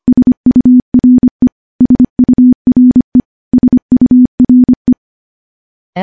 Morse code, heard as a keyed tone, about 25 wpm, 250 Hz. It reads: SURE SURE SURE
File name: morse / morse-training-25WPM